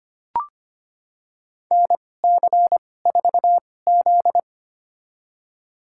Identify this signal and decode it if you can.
Morse code, heard as a keyed tone, about 25 wpm, spaced as for 18 wpm, 700 Hz. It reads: NC4Z
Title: USA Call Signs - 2x Character Spacing - 25wpm